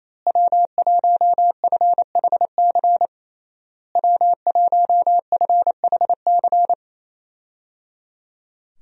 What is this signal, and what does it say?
Morse code, heard as a keyed tone, about 28 wpm, 700 Hz. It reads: W1FHC W1FHC
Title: USA Call Signs - 5 Characters - Repeated 3x - 28wpm